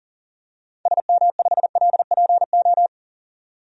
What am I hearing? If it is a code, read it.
Morse code, heard as a keyed tone, about 38 wpm, 700 Hz. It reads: SM5LPO
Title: DXCC-100 Call Signs - 6 Characters - 40wpm